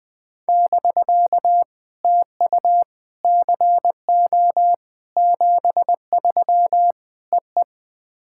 Morse code, heard as a keyed tone, about 20 wpm, 700 Hz. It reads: <BK> TU CO 73 EE
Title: POTA QSO Practice - 20wpm